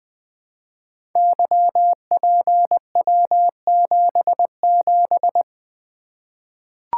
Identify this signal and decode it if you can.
Morse code, heard as a keyed tone, about 20 wpm, 700 Hz. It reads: YPW77
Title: Code Groups - 5 Letter:Numbers - 20wpm